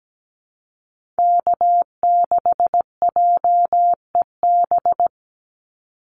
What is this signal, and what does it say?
Morse code, heard as a keyed tone, about 17 wpm, 700 Hz. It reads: K6JEB
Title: Call Signs Repeated 3x - 17wpm